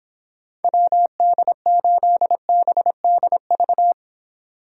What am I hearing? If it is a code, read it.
Morse code, heard as a keyed tone, about 26 wpm, 700 Hz. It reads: WD8BDV